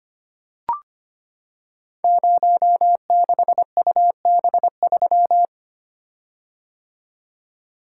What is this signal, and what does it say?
Morse code, heard as a keyed tone, about 25 wpm, 700 Hz. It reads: 06UB3
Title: Code Groups - 5 Letter:Numbers - 25wpm